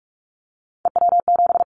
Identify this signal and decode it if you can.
Morse code, heard as a keyed tone, about 44 wpm, 700 Hz. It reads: EW7
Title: Callsign Prefixes for Top 100 DXCC - 45wpm